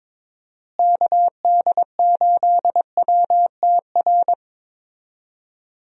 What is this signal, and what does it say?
Morse code, heard as a keyed tone, about 22 wpm, 700 Hz. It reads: KD8WTR
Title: USA Call Signs - 6 Characters - Repeated 3x - 22wpm